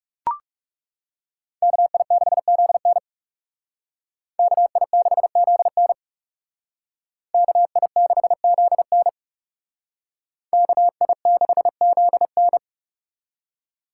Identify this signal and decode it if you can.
Morse code, heard as a keyed tone, about 40 wpm, 700 Hz. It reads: KI6ZN KI6ZN KI6ZN KI6ZN